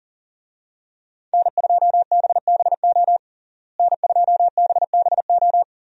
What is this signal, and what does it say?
Morse code, heard as a keyed tone, about 40 wpm, 700 Hz. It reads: N2BBO N2BBO
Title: USA Call Signs - 5 Characters - Repeated 3x - 40wpm